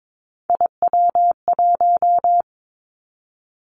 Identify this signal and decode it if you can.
Morse code, heard as a keyed tone, about 22 wpm, 700 Hz. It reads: IW1
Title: Callsign Prefixes for Top 25 DXCC - Repeated 3x - 22wpm